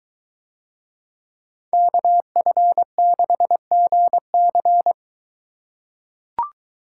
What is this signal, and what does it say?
Morse code, heard as a keyed tone, about 22 wpm, 700 Hz. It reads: KF6GC